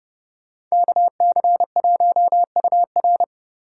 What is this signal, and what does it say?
Morse code, heard as a keyed tone, about 36 wpm, 700 Hz. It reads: KC1UR
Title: Speed Racing - USA Call Signs 30wpm